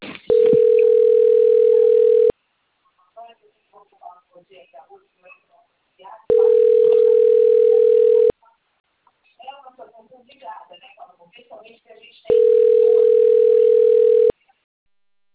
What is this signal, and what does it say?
Ringback tone (North America), 440 + 480 Hz, with speech.
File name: out- - - - - .44891